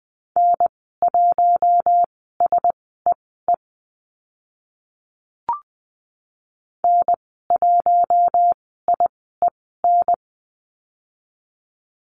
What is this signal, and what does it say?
Morse code, heard as a keyed tone, about 20 wpm, spaced as for 15 wpm, 700 Hz. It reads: N1SEE N1IEN
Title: 042 Lesson 8g - Number 1 - Call Signs 20wpm